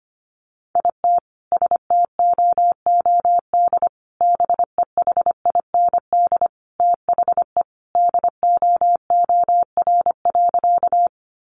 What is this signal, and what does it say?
Morse code, heard as a keyed tone, about 25 wpm, 700 Hz. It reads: IT STOOD BEHIND THE DOOR.
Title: Sentences from Top 500 Words 25wpm